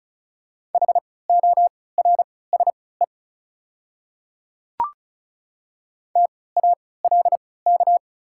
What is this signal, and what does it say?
Morse code, heard as a keyed tone, about 35 wpm, spaced as for 20 wpm, 700 Hz. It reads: HORSE TALK
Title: 3x Character Spacing - Top 500 Words - 35wpm